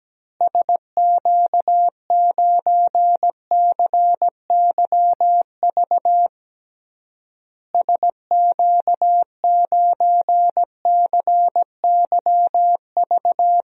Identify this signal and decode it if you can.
Morse code, heard as a keyed tone, about 17 wpm, 700 Hz. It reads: SQ9CYV SQ9CYV
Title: DXCC-100 Call Signs - 6 Characters - Repeated 3x - 17wpm